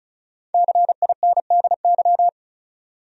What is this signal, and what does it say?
Morse code, heard as a keyed tone, about 36 wpm, 700 Hz. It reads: CINDY
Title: USA Common Names - 35wpm